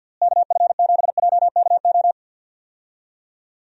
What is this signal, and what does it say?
Morse code, heard as a keyed tone, about 52 wpm, 700 Hz. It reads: KF7JKO
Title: USA Call Signs - 6 Characters - Repeated 2x - 50wpm